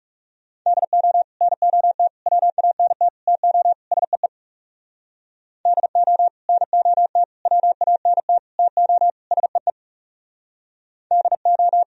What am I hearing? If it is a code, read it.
Morse code, heard as a keyed tone, about 38 wpm, 700 Hz. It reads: DO NOT WANT TO SEE DO NOT WANT TO SEE DO